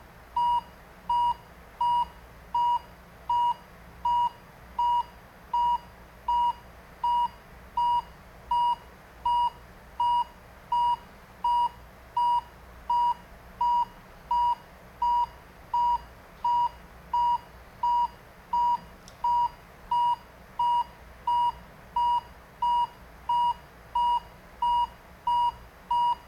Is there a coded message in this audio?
No.